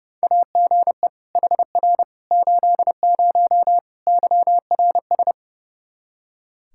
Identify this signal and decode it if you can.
Morse code, heard as a keyed tone, about 30 wpm, 700 Hz. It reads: AGE HR 80 YRS
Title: K1USN SST QSO Practice 30wpm